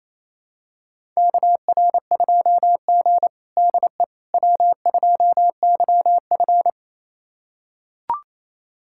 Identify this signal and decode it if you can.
Morse code, heard as a keyed tone, about 28 wpm, 700 Hz. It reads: KR2G DE W2YF
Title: USA Callsign DE Callsign - 4 Characters - 28wpm